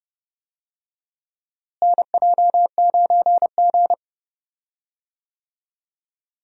Morse code, heard as a keyed tone, about 31 wpm, 700 Hz. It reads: NJ9G
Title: USA Call Signs - 4 Characters - Repeated 2x - 30wpm